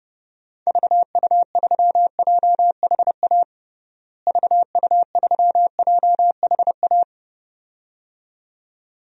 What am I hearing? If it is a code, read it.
Morse code, heard as a keyed tone, about 31 wpm, 700 Hz. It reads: VU3JHA VU3JHA
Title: DXCC-100 Call Signs - 6 Characters - Repeated 2x - 30wpm